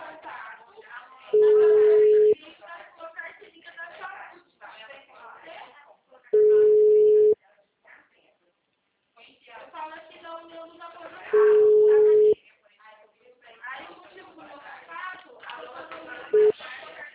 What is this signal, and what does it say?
Ringback tone (Europe, 425 Hz), with speech.